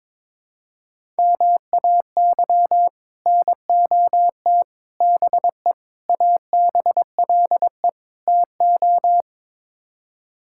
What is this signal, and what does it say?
Morse code, heard as a keyed tone, about 21 wpm, 700 Hz. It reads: MAY NOT BE ABLE TO
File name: practice / 22wpm / Build A Phrase - 5-Word Phrases - 22wpm